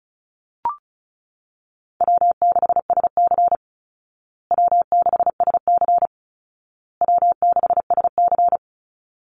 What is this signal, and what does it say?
Morse code, heard as a keyed tone, about 34 wpm, 700 Hz. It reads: W6SC W6SC W6SC
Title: Easy Call Signs Repeated 3x - 35wpm